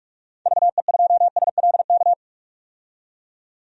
Morse code, heard as a keyed tone, about 42 wpm, 700 Hz. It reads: VE2SLK
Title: DXCC-100 Call Signs - 6 Characters - 45wpm